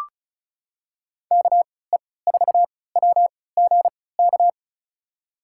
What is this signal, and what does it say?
Morse code, heard as a keyed tone, about 36 wpm, spaced as for 20 wpm, 700 Hz. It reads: KE4WGK
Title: USA Call Signs - 6 Characters - 3x Character Spacing - 35wpm